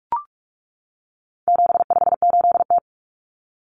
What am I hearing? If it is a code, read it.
Morse code, heard as a keyed tone, about 44 wpm, 700 Hz. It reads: 758T
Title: Common Cut Numbers - 4-Digit 45wpm